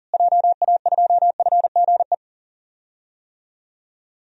Morse code, heard as a keyed tone, about 37 wpm, 700 Hz. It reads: JA2FGE
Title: DXCC-100 Call Signs - 6 Characters - Repeated 2x - 40wpm